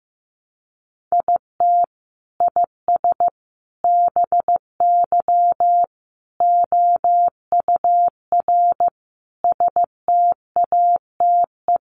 Morse code, heard as a keyed tone, about 15 wpm, 700 Hz. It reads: IT IS BY OUR STATE
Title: Sentences from Top 1400 Words 15wpm